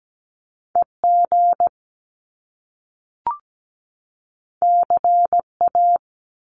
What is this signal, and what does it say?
Morse code, heard as a keyed tone, about 17 wpm, 700 Hz. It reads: EG CA
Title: Top Two Letter Combinations in Words 17wpm